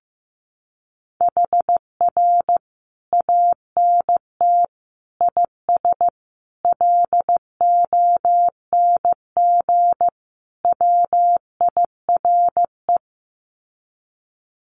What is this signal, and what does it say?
Morse code, heard as a keyed tone, about 15 wpm, 700 Hz. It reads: HR ANT IS LONG WIRE
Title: Common QSO Phrases 15wpm